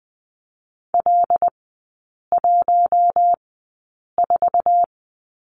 Morse code, heard as a keyed tone, about 20 wpm, spaced as for 8 wpm, 700 Hz. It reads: L14
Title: 166 Lesson 27e - Number 7 - Three Characters 20wpm